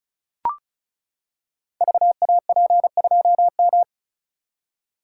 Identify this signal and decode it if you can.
Morse code, heard as a keyed tone, about 36 wpm, 700 Hz. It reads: VAP2M
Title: Code Groups - 5 Letter:Numbers - 35wpm